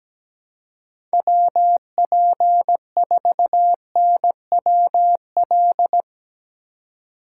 Morse code, heard as a keyed tone, about 19 wpm, 700 Hz. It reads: WP4NWL
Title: Speed Racing - USA 2x3 Call Signs 17wpm